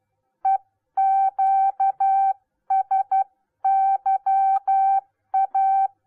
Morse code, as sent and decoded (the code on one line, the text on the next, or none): . --.- ... -.-- .-
EQSYA